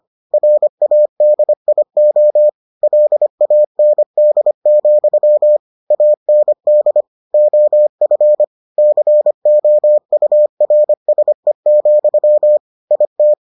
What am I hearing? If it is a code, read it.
Morse code, heard as a keyed tone, about 25 wpm, 600 Hz. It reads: RADIO LAND, AND OF COURSE, IT